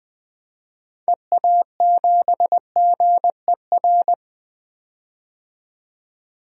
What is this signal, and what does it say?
Morse code, heard as a keyed tone, about 20 wpm, 700 Hz. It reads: EA7GER
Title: DXCC-100 Call Signs - 6 Characters - 20wpm